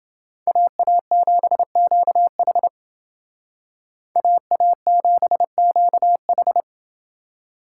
Morse code, heard as a keyed tone, about 28 wpm, 700 Hz. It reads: AA7QH AA7QH